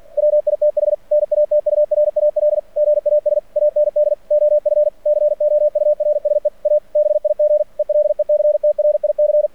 Morse code, heard as a keyed tone, about 46 wpm, 600 Hz. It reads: WITH NATURAL GAS AND OF COURSE A BIG ELECTRIC
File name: archived / CRSnet